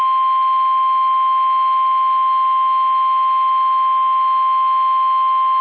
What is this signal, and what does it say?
Emergency Alert System (SAME) tones (the 1050 Hz attention tone). It is heard from 0.8 to 5.0 s.
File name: eas / KUSO / KUSO